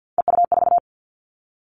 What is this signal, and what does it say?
Morse code, heard as a keyed tone, about 50 wpm, 700 Hz. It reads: EU4
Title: All Cut Numbers - 3-Digit 50wpm